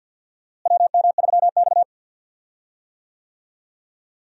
Morse code, heard as a keyed tone, about 52 wpm, 700 Hz. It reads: WM3X
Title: USA Call Signs - 4 Characters - Repeated 3x - 50wpm